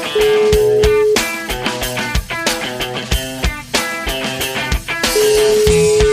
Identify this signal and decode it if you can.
Ringback tone (Europe, 425 Hz), with noise.